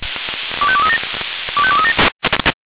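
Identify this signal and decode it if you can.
Five-tone selective call (ZVEI), heard as quick runs of five tones, tones of about 70 ms. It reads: 25237 25237
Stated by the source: ZVEI five-tone call received on an SDR.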